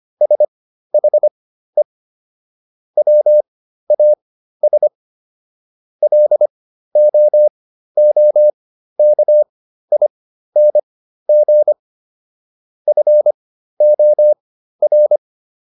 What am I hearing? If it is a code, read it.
Morse code, heard as a keyed tone, about 25 wpm, spaced as for 13 wpm, 600 Hz. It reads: SHE WAS LOOKING FOR